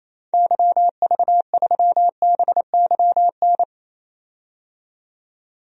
Morse code, heard as a keyed tone, about 28 wpm, 700 Hz. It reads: YV3BYN
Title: DXCC-100 Call Signs - 6 Characters - 28wpm